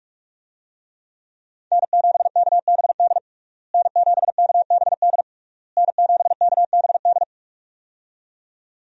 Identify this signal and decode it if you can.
Morse code, heard as a keyed tone, about 46 wpm, 700 Hz. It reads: N7KBD N7KBD N7KBD
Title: USA Call Signs - 5 Characters - Repeated 3x - 45wpm